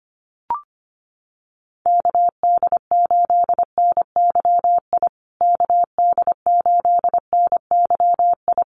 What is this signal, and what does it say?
Morse code, heard as a keyed tone, about 25 wpm, 700 Hz. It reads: KD8NYI KD8NYI
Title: USA Call Signs Repeated 3x 25wpm